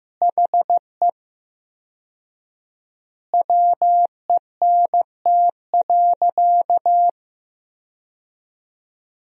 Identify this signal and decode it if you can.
Morse code, heard as a keyed tone, about 15 wpm, 700 Hz. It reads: HE WENT.
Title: 4x Word-Spacing - Sentences from Top 200 Words - 15wpm